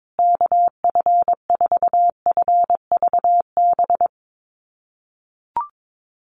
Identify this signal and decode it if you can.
Morse code, heard as a keyed tone, about 22 wpm, 700 Hz. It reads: KF4FVB
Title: USA Call Signs Repeated 3x 22wpm